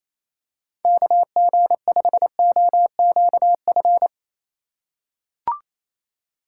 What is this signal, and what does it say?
Morse code, heard as a keyed tone, about 28 wpm, 700 Hz. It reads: KG5OQF
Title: USA Call Signs - 6 Characters - 3x Character Spacing - 28wpm